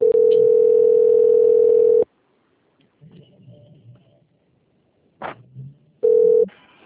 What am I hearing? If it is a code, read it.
Ringback tone (North America), 440 + 480 Hz.